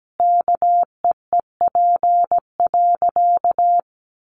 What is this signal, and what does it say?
Morse code, heard as a keyed tone, about 17 wpm, 700 Hz. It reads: KEEP.
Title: Sentences from Top 1100 Words 17wpm